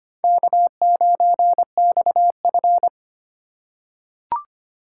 Morse code, heard as a keyed tone, about 25 wpm, 700 Hz. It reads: K9XF
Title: USA Call Signs - 4 Characters - 4x Character Spacing - 25wpm